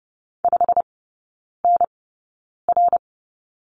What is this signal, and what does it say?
Morse code, heard as a keyed tone, about 30 wpm, spaced as for 9 wpm, 700 Hz. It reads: 5NR
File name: course / 30:10wpm / 087 Lesson 15e - Number 5 - Three Characters 30wpm